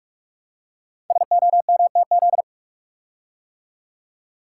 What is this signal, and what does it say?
Morse code, heard as a keyed tone, about 44 wpm, 700 Hz. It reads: SOMTZ
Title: Code Groups - 5 Letter:Numbers - 45wpm